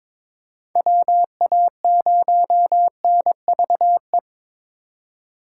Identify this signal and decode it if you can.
Morse code, heard as a keyed tone, about 22 wpm, 700 Hz. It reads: WA0NVE
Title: USA Call Signs - 6 Characters - 2x Character Spacing - 22wpm